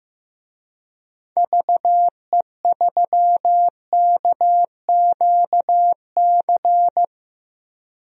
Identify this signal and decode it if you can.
Morse code, heard as a keyed tone, about 15 wpm, 700 Hz. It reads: VE3KQC
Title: DXCC-100 Call Signs - 6 Characters - 15wpm